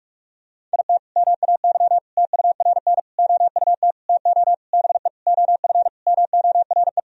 Morse code, heard as a keyed tone, about 44 wpm, 700 Hz. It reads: IT MAY TURN OUT TO BE OF MORE
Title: Sentences from Top 1000 Words - Review Part 2 - 45wpm